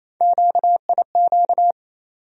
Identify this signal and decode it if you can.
Morse code, heard as a keyed tone, about 28 wpm, 700 Hz. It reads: QIQ
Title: USA Call Sign Postfixes - 3 Letters 28wpm